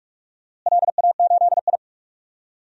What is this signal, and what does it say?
Morse code, heard as a keyed tone, about 42 wpm, 700 Hz. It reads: LA8I
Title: DXCC-100 Call Signs - 4 Characters - Repeated 2x - 45wpm